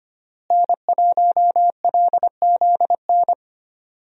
Morse code, heard as a keyed tone, about 25 wpm, 700 Hz. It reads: N1LZN
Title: USA Call Signs - 5 Characters - Repeated 3x - 25wpm